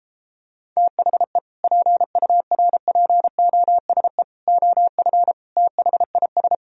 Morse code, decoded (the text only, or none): THE PURPOSE OF THIS